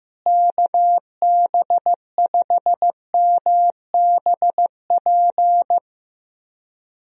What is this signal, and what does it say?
Morse code, heard as a keyed tone, about 15 wpm, 700 Hz. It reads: KB5MBP
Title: USA Call Signs - 6 Characters - Repeated 2x - 15wpm